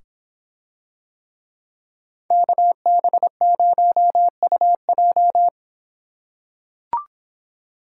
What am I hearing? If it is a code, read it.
Morse code, heard as a keyed tone, about 24 wpm, 700 Hz. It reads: KB0UJ